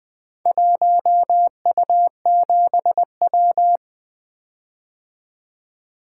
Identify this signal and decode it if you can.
Morse code, heard as a keyed tone, about 20 wpm, 700 Hz. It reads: 1U7W
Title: Four Letter-Number Combinations - 20wpm